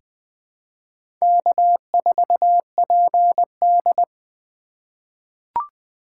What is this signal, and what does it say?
Morse code, heard as a keyed tone, about 20 wpm, 700 Hz. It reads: K4PD
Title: USA Call Signs - 4 Characters - 3x Character Spacing - 20wpm